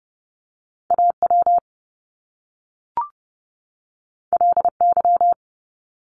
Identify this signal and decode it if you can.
Morse code, heard as a keyed tone, about 30 wpm, 700 Hz. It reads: AW LY
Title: Top Two Letter Combinations in Words 30wpm